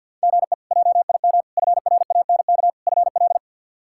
Morse code, heard as a keyed tone, about 51 wpm, 700 Hz. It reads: GE JIM FRANK FL
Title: K1USN SST QSO Practice - 50wpm